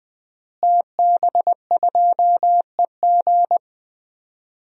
Morse code, heard as a keyed tone, about 20 wpm, 700 Hz. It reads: TB2EG
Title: DXCC-100 Call Signs - 5 Characters - Repeated 3x - 20wpm